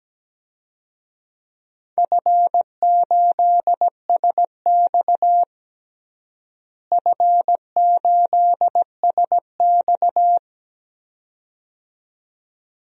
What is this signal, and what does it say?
Morse code, heard as a keyed tone, about 17 wpm, 700 Hz. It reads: F8SX F8SX
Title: DXCC-100 Call Signs - 4 Characters - Repeated 2x - 17wpm